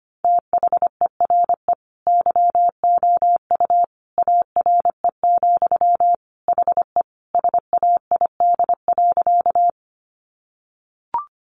Morse code, heard as a keyed tone, about 25 wpm, 700 Hz. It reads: THERE YOU ARE, HE SAID.
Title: Sentences from Top 100 Words 25wpm